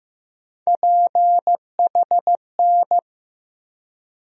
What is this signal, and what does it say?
Morse code, heard as a keyed tone, about 15 wpm, 700 Hz. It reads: PHN
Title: USA Call Sign Postfixes - 3 Letters 15wpm